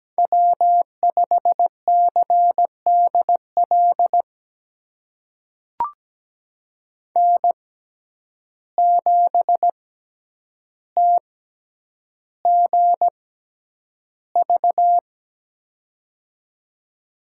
Morse code, heard as a keyed tone, about 17 wpm, spaced as for 9 wpm, 700 Hz. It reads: W5CDL N7TGV
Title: USA Call Signs - 5 Characters - 6x Character Spacing - 17wpm